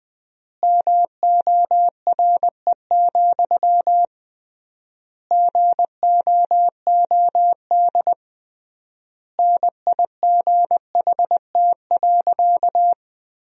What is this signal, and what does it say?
Morse code, heard as a keyed tone, about 20 wpm, 700 Hz. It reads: MORE, GOOD NIGHT.
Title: 3x Word-Spacing - Sentences from Top 300 Words - 20wpm